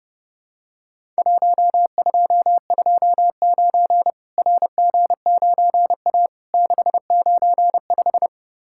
Morse code, heard as a keyed tone, about 32 wpm, 700 Hz. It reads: 1229 RG9A 695
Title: Worked All Europe DX Contest - QTC Exchange - 30wpm